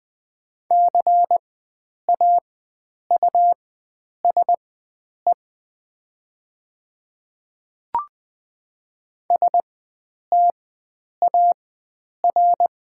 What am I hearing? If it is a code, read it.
Morse code, heard as a keyed tone, about 20 wpm, spaced as for 9 wpm, 700 Hz. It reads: CAUSE STAR
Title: 4x Character Spacing - Top 500 Words - 20wpm